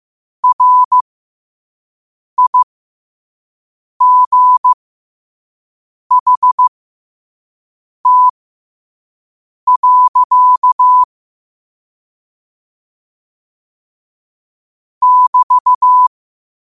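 Morse code, heard as a keyed tone, about 15 wpm, spaced as for 5 wpm, 1000 Hz. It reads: RIGHT. <BT>
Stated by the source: Morse code quote of the day sent at 5 WPM.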